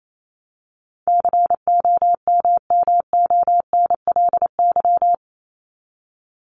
Morse code, heard as a keyed tone, about 28 wpm, 700 Hz. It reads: COMMONLY
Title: LY Suffix - Top 100 Words with Base Words 28wpm